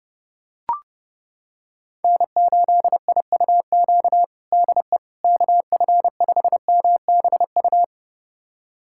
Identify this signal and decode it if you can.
Morse code, heard as a keyed tone, about 32 wpm, 700 Hz. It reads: N8IUQ DE KF5MBU
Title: USA Callsign DE Callsign 30wpm